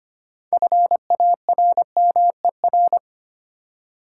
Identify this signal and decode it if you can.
Morse code, heard as a keyed tone, about 25 wpm, 700 Hz. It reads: FARMER
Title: Six Letter Words - 25wpm